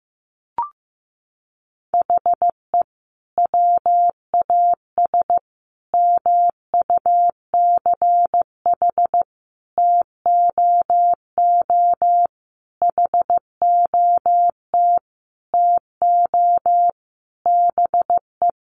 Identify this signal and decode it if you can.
Morse code, heard as a keyed tone, about 15 wpm, 700 Hz. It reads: HE WAS MUCH TOO HOT TO BE
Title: Sentences from Top 1200 Words 15wpm